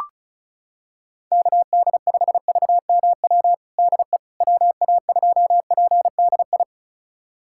Morse code, heard as a keyed tone, about 36 wpm, 700 Hz. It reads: KD5VMW DE WA2PDI
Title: USA Callsign DE Callsign - 6 Characters - 35wpm